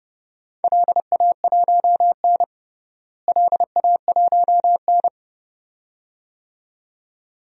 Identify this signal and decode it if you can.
Morse code, heard as a keyed tone, about 30 wpm, 700 Hz. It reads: LA1N LA1N
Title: DXCC-100 Call Signs - 4 Characters - Repeated 2x - 30wpm